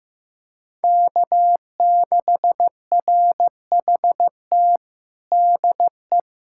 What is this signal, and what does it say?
Morse code, heard as a keyed tone, about 15 wpm, 700 Hz. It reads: K6RHT DE
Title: Practice QSO - Exchange 4 - Ending - 15wpm